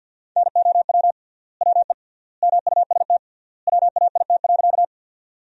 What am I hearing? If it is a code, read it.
Morse code, heard as a keyed tone, about 53 wpm, 700 Hz. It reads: NOW WE MUST WAIT.
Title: 3x Word-Spacing - Sentences from Top 500 Words - 50wpm